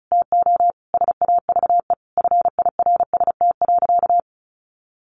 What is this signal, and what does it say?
Morse code, heard as a keyed tone, about 34 wpm, 700 Hz. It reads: TO SAVE FIRST.
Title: Sentences from Top 1100 Words 35wpm